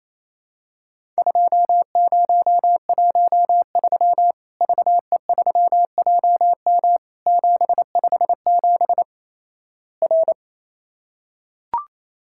Morse code, heard as a keyed tone, about 28 wpm, 700 Hz. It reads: 2013 VE3JM 757 R
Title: Worked All Europe DX Contest - QTC Exchange - 28wpm